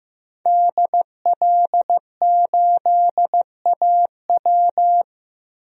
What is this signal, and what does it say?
Morse code, heard as a keyed tone, about 15 wpm, 700 Hz. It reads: DL8AW
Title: DXCC-100 Call Signs - 5 Characters - Repeated 2x - 15wpm